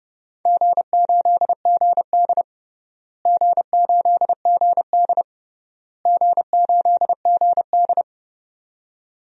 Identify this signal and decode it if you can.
Morse code, heard as a keyed tone, about 30 wpm, 700 Hz. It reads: G8GD G8GD G8GD
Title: DXCC-100 Call Signs - 4 Characters - Repeated 3x - 30wpm